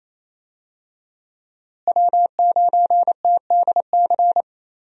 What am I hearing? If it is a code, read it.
Morse code, heard as a keyed tone, about 28 wpm, 700 Hz. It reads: W9TDC
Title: USA Call Signs - 5 Characters - Repeated 2x - 28wpm